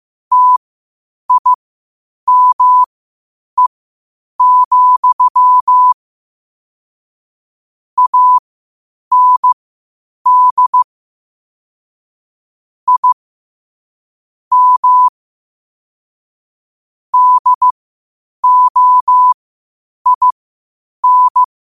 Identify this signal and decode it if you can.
Morse code, heard as a keyed tone, about 15 wpm, spaced as for 8 wpm, 1000 Hz. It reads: TIME, AND I M DOIN
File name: qotdfiles / 8wpm